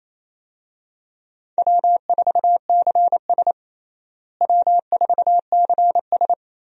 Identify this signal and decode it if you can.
Morse code, heard as a keyed tone, about 28 wpm, 700 Hz. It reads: W4CS W4CS
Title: USA Call Signs - 4 Characters - Repeated 2x - 28wpm